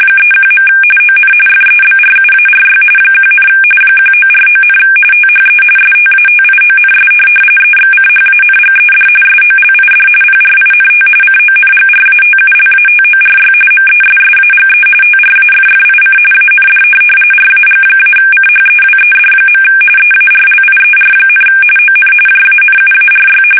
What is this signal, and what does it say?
RTTY, about 45.45 baud. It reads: 31. ON FEBRUARY 1, 1949, THE STATIONS EXCHANGED SOLID PRINT CONGRATULATORY MESSAGE TRAFFIC AND RAG-CHEWED. EARLIER, ON JANUARY
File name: fsk850 / teletype-RTTY